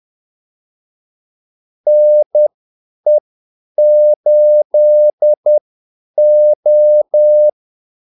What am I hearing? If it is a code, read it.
Morse code, heard as a keyed tone, about 10 wpm, spaced as for 8 wpm, 600 Hz. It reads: NE8O
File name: Callsigns-10-8wpm